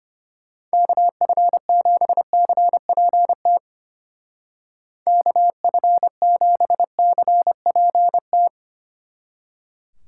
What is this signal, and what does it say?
Morse code, heard as a keyed tone, about 30 wpm, 700 Hz. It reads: KF7CPT KF7CPT
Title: Speed Racing - USA Call Signs 25wpm